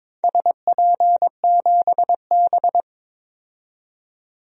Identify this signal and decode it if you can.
Morse code, heard as a keyed tone, about 22 wpm, 700 Hz. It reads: SP7B
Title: DXCC-100 Call Signs - 4 Characters - 22wpm